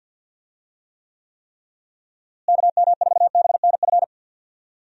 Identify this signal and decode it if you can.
Morse code, heard as a keyed tone, about 51 wpm, 700 Hz. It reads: KM4BNF
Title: USA Call Signs - 6 Characters - Repeated 3x - 50wpm